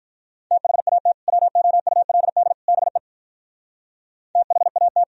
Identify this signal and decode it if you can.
Morse code, heard as a keyed tone, about 46 wpm, 690 Hz. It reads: THAT WOULD BE THAT